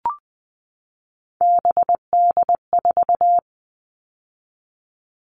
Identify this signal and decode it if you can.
Morse code, heard as a keyed tone, about 20 wpm, 700 Hz. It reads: BD4
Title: Callsign Prefixes for Top 100 DXCC - 20wpm